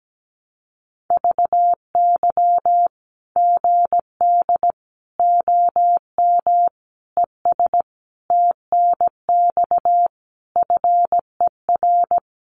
Morse code, heard as a keyed tone, about 17 wpm, 700 Hz. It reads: VY GD OM ES TNX FER
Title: Common QSO Phrases 17wpm